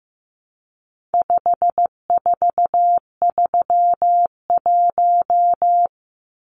Morse code, heard as a keyed tone, about 15 wpm, 700 Hz. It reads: 5431
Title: Numbers - 4-Digit 15wpm